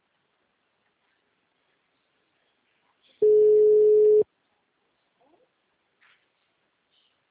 Ringback tone (Europe, 425 Hz).